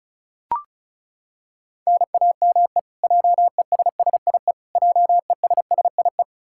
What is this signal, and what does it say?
Morse code, heard as a keyed tone, about 36 wpm, 700 Hz. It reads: NAME JESSIE JESSIE
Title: Practice QSO - Exchange 1 - Beginning - 35wpm